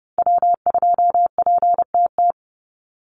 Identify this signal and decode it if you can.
Morse code, heard as a keyed tone, about 30 wpm, 700 Hz. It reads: W2PTT
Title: Call Signs Repeated 2x - 30wpm